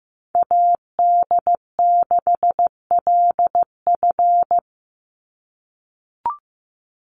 Morse code, heard as a keyed tone, about 15 wpm, 700 Hz. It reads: AD6LF
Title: USA Call Signs 15wpm